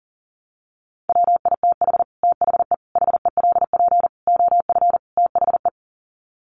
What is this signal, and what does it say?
Morse code, heard as a keyed tone, about 40 wpm, 700 Hz. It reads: WITH THE HELP OF THE
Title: Sets of 5 Words Encore 40wpm